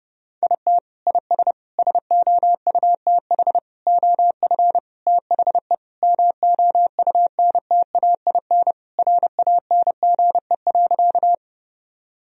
Morse code, heard as a keyed tone, about 30 wpm, 700 Hz. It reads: IT IS SOUTH OF THE MOUNTAIN RANGE.
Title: Sentences from Top 1000 Words - Review Part 4 - 30wpm